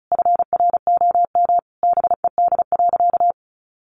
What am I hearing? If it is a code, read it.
Morse code, heard as a keyed tone, about 34 wpm, 700 Hz. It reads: FROM BED.
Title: Sentences from Top 900 Words 35wpm